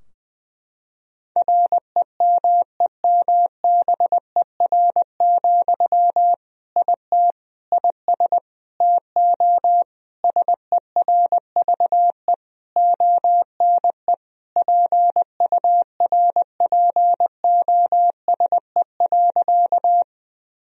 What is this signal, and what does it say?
Morse code, heard as a keyed tone, about 21 wpm, 700 Hz. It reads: REMEMBER, IT IS TO SERVE ONE PURPOSE.
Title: Sentences from Top 1000 Words - Review Part 2 - 20wpm